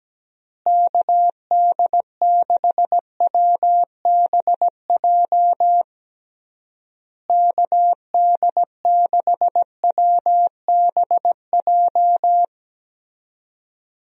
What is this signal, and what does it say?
Morse code, heard as a keyed tone, about 17 wpm, 700 Hz. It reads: KD6WBJ KD6WBJ
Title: USA Call Signs - 6 Characters - Repeated 3x - 17wpm